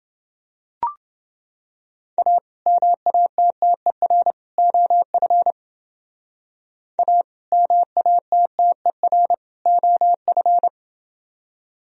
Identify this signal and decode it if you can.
Morse code, heard as a keyed tone, about 28 wpm, 700 Hz. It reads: A MATTER OF A MATTER OF